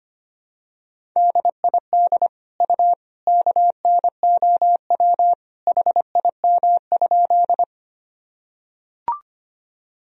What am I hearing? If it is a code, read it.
Morse code, heard as a keyed tone, about 25 wpm, 700 Hz. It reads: DID U KNOW HIM?
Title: Sentences from Top 100 Words and Abbreviations 25wpm